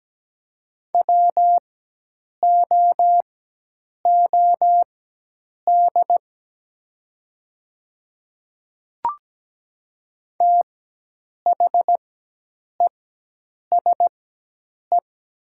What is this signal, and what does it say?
Morse code, heard as a keyed tone, about 17 wpm, spaced as for 8 wpm, 700 Hz. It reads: WOOD THESE
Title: 4x Character Spacing - Top 300 Words - 17wpm